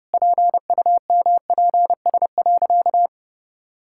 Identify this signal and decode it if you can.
Morse code, heard as a keyed tone, about 30 wpm, 700 Hz. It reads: PUMPS.
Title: Pangram Challenge v2 - 30wpm